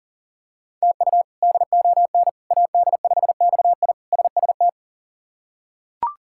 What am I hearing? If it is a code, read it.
Morse code, heard as a keyed tone, about 40 wpm, 700 Hz. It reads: TU DON AD5XI SST
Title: K1USN SST QSO Practice - 40wpm